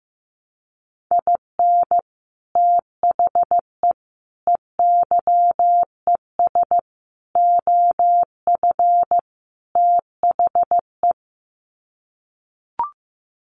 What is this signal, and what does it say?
Morse code, heard as a keyed tone, about 15 wpm, 700 Hz. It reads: IN THE EYES OF THE
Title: Sets of 5 Words 15wpm